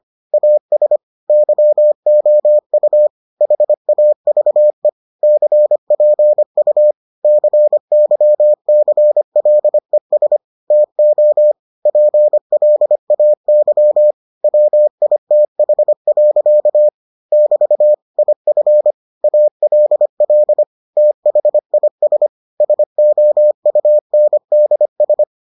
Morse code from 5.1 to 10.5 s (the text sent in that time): CPU CYCLES